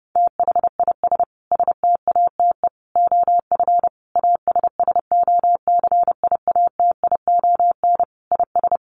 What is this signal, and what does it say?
Morse code, heard as a keyed tone, about 30 wpm, 700 Hz. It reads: THIS STATE OF ASSOCIATION IS
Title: Sentences from Top 1100 Words 30wpm